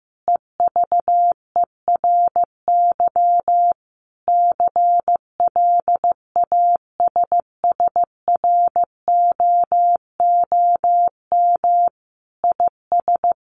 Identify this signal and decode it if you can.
Morse code, heard as a keyed tone, about 15 wpm, 700 Hz. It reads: EVERY CLASSROOM IS
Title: Sentences from Top 1500 Words 15wpm